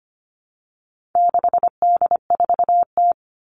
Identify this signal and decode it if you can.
Morse code, heard as a keyed tone, about 25 wpm, 700 Hz. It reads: 6D4T
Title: All Cut Numbers - 4-Digit 25wpm